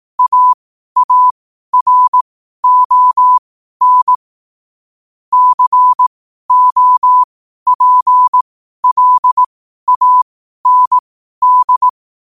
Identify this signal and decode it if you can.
Morse code, heard as a keyed tone, about 18 wpm, spaced as for 12 wpm, 1000 Hz. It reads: AARON COPLAND
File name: qotdfiles / 12wpm